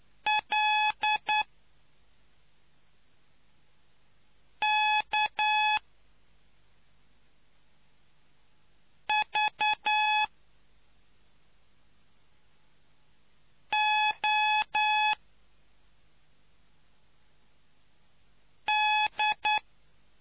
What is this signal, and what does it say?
Morse code, heard as a keyed tone, about 9 wpm, spaced as for 2 wpm, 850 Hz. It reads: LKVOD